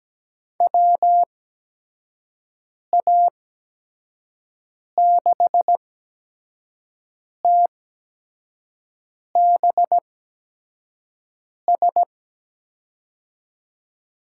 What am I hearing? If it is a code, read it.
Morse code, heard as a keyed tone, about 17 wpm, spaced as for 5 wpm, 700 Hz. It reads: WA6TBS
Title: USA Call Signs - 6 Characters - 8x Character Spacing - 17wpm